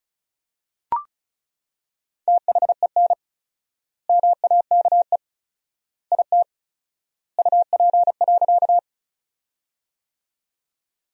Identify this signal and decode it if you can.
Morse code, heard as a keyed tone, about 36 wpm, 700 Hz. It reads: THEN MAKE IT UP.
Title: 4x Word-Spacing - Sentences from Top 100 Words - 35wpm